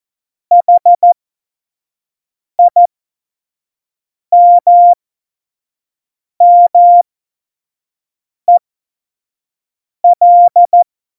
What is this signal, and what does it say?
Morse code, heard as a keyed tone, about 14 wpm, spaced as for 5 wpm, 700 Hz. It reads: HIMMEL
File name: Morsekurs / Witze / Witz 22. Dezember 2020, 25 F70